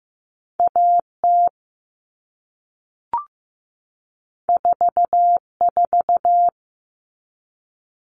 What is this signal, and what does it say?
Morse code, heard as a keyed tone, about 15 wpm, 700 Hz. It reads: AT 44
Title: All Cut Numbers - 2-Digit 15wpm